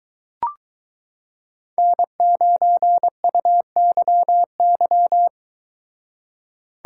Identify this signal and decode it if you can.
Morse code, heard as a keyed tone, about 22 wpm, 700 Hz. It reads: N9UYY